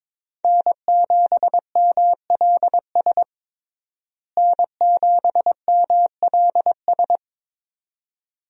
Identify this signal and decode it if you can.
Morse code, heard as a keyed tone, about 22 wpm, 700 Hz. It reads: N7MLS N7MLS
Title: USA Call Signs - 5 Characters - Repeated 2x - 22wpm